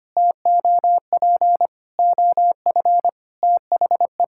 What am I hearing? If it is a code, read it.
Morse code, heard as a keyed tone, about 25 wpm, 700 Hz. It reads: TOP OF THE
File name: practice / 25wpm / Sentences with 3-Word Phrases - 25wpm